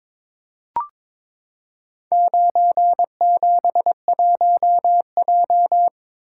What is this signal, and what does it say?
Morse code, heard as a keyed tone, about 23 wpm, 700 Hz. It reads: 971J
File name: practice / 22wpm / Four Letter-Number Combinations - 22wpm